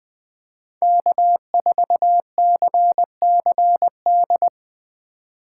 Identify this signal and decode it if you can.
Morse code, heard as a keyed tone, about 20 wpm, 700 Hz. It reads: K4CCD
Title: USA Call Signs - 5 Characters - 2x Character Spacing - 20wpm